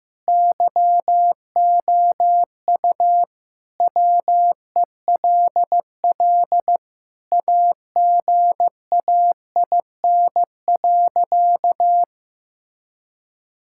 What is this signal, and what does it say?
Morse code, heard as a keyed tone, about 16 wpm, 700 Hz. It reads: YOU WELL AGAIN.
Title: Sentences from Top 1600 Words 15wpm